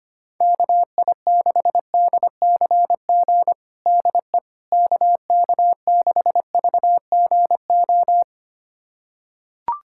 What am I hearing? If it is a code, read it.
Morse code, heard as a keyed tone, about 25 wpm, 700 Hz. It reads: KI6DCG DE KK6VGO
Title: USA Callsign DE Callsign - 6 Characters - 25wpm